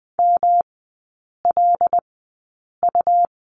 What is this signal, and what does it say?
Morse code, heard as a keyed tone, about 20 wpm, spaced as for 8 wpm, 700 Hz. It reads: MLU
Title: 152 Lesson 25e - Letter p - Three Characters 20wpm